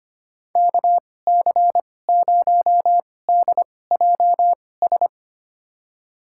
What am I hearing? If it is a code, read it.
Morse code, heard as a keyed tone, about 25 wpm, spaced as for 18 wpm, 700 Hz. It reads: KC0DJS
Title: USA Call Signs - 6 Characters - 2x Character Spacing - 25wpm